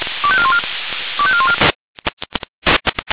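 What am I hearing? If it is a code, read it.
Five-tone selective call (ZVEI), heard as quick runs of five tones, tones of about 70 ms. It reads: 25414 25414
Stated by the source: ZVEI five-tone selective call received by SDR.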